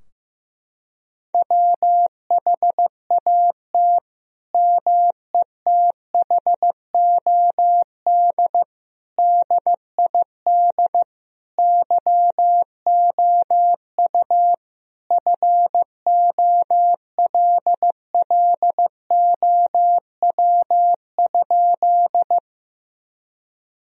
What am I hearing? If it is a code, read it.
Morse code, heard as a keyed tone, about 16 wpm, 700 Hz. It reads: WHAT METHOD DID YOU FOLLOW?
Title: Sentences from Top 1000 Words - Review Part 3 - 15wpm